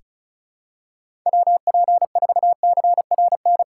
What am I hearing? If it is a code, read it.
Morse code, heard as a keyed tone, about 34 wpm, 700 Hz. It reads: WP4CRN
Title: USA Call Signs - 6 Characters - 2x Character Spacing - 35wpm